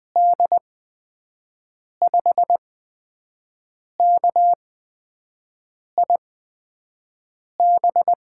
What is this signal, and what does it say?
Morse code, heard as a keyed tone, about 20 wpm, spaced as for 5 wpm, 700 Hz. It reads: D5KIB
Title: USA Call Signs - 6 Characters - 8x Character Spacing - 20wpm